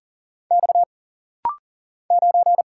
Morse code, heard as a keyed tone, about 40 wpm, 700 Hz. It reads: X 9
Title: Single Letter-Number Alt - Rapid-Fire 40wpm